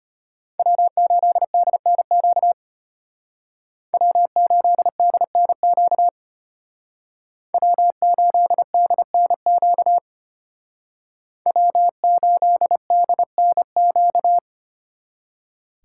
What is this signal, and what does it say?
Morse code, heard as a keyed tone, about 32 wpm, 700 Hz. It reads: W8DNQ W8DNQ W8DNQ W8DNQ